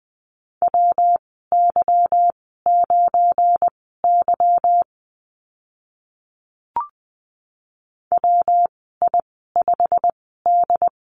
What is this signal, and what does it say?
Morse code, heard as a keyed tone, about 20 wpm, spaced as for 14 wpm, 700 Hz. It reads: WY9Y WI5D
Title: 179 Lesson 29g - Slash - Call Signs 20wpm